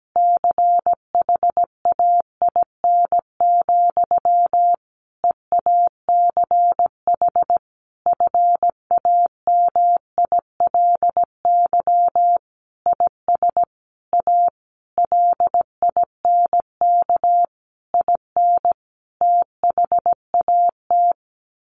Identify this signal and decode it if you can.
Morse code, heard as a keyed tone, about 17 wpm, 700 Hz. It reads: CHAIN, EACH FAMILY IS A LINK IN THAT
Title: Sentences from Top 1500 Words 17wpm